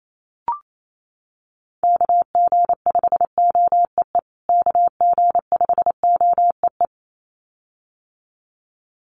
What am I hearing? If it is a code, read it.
Morse code, heard as a keyed tone, about 28 wpm, 700 Hz. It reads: KG5OEE KG5OEE
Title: USA Call Signs Repeated 2x 28wpm